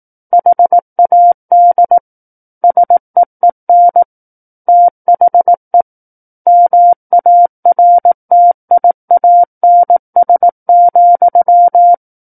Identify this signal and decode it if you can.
Morse code, heard as a keyed tone, about 18 wpm, 700 Hz. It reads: HAD SEEN THE MARTIANS,
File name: Book One / 09-Chapter 9 - 18 WPM